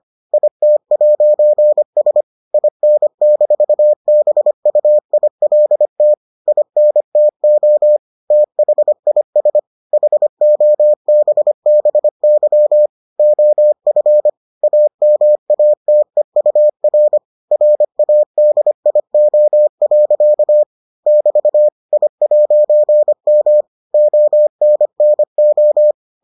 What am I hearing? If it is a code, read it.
Morse code, heard as a keyed tone, about 25 wpm, 600 Hz. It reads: IT'S IN-BUILT INTO THIS HOBBY OF AMATEUR RADIO. <BT> I'M ONNO